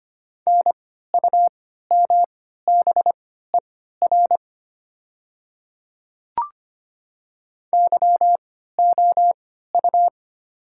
Morse code, heard as a keyed tone, about 25 wpm, spaced as for 14 wpm, 700 Hz. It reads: NUMBER YOU
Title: 3x Character Spacing - Top 100 Words - 25wpm